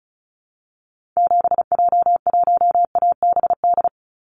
Morse code, heard as a keyed tone, about 36 wpm, 700 Hz. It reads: 7J1ABD
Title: Call Signs - 35wpm